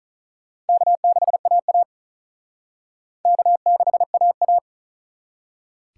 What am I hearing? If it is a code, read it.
Morse code, heard as a keyed tone, about 44 wpm, 700 Hz. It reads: K6AA K6AA